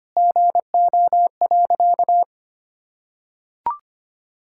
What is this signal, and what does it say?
Morse code, heard as a keyed tone, about 25 wpm, 700 Hz. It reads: GO.
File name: practice / 25wpm / Sentences with 2-Word Phrases - 25wpm